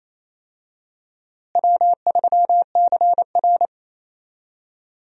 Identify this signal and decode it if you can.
Morse code, heard as a keyed tone, about 28 wpm, 700 Hz. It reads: W3CR
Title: USA Call Signs - 4 Characters - Repeated 2x - 28wpm